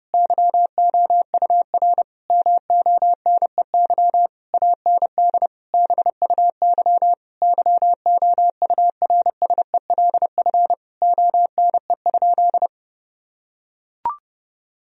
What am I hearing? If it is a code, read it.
Morse code, heard as a keyed tone, about 30 wpm, 700 Hz. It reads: YOUR MONEY AND BUY YOURSELF ONE?
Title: Sentences from Top 1000 Words - Review Part 4 - 30wpm